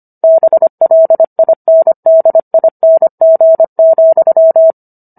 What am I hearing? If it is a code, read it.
Morse code, heard as a keyed tone, about 25 wpm, 650 Hz. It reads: BLINDING,